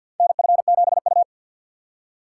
Morse code, heard as a keyed tone, about 50 wpm, 700 Hz. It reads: NF7U
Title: USA Call Signs - 3x Character Spacing - 50wpm